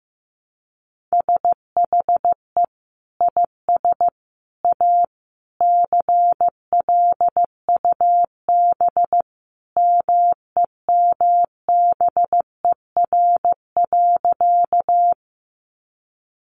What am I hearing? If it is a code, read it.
Morse code, heard as a keyed tone, about 15 wpm, 700 Hz. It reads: SHE IS A CLUB MEMBER.
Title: Sentences from Top 1200 Words 15wpm